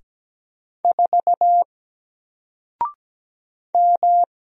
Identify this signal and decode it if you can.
Morse code, heard as a keyed tone, about 17 wpm, 700 Hz. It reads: 4 M
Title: Single Letter-Number Alt 17wpm